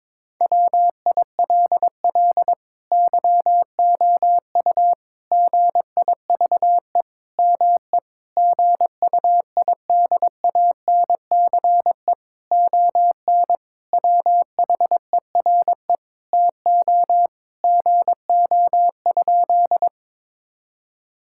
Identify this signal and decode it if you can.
Morse code, heard as a keyed tone, about 22 wpm, 700 Hz. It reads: WILL YOU GIVE ME GUIDANCE ON WHERE TO GO?
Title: Sentences from Top 1600 Words 22wpm